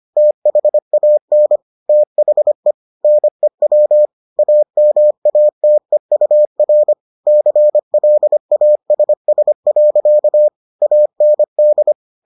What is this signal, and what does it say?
Morse code, heard as a keyed tone, about 25 wpm, 600 Hz. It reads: THAN THE NEW AMATEUR CLASS. AND